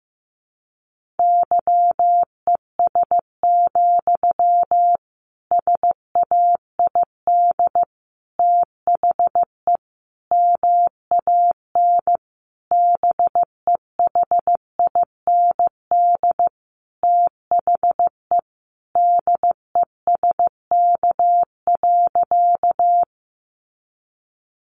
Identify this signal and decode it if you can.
Morse code, heard as a keyed tone, about 15 wpm, 700 Hz. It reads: YES, SAID THE MAN BEHIND THE DESK.
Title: A Sound Of Thunder 15wpm